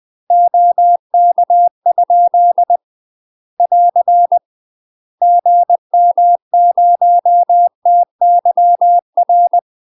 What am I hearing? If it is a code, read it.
Morse code, heard as a keyed tone, about 20 wpm, 700 Hz. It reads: OK? <AR> GM0TYR
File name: UK MASTER QSOs TRAINING / 20WPM / UK MASTER QSOs 0024